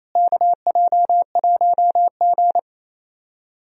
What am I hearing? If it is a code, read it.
Morse code, heard as a keyed tone, about 28 wpm, 700 Hz. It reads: KJ1G
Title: USA Call Signs - 4 Characters - Repeated 3x - 28wpm